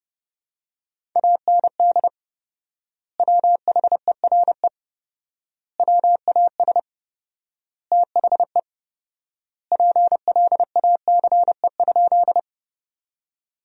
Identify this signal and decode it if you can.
Morse code, heard as a keyed tone, about 32 wpm, 700 Hz. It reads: AND WHERE WAS THE PLACE?
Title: 4x Word-Spacing - Sentences from Top 200 Words - 30wpm